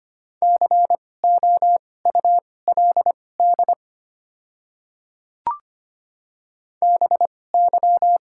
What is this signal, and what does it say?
Morse code, heard as a keyed tone, about 25 wpm, spaced as for 18 wpm, 690 Hz. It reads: COULD BY
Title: 2x Character Spacing - Top 100 Words - 25wpm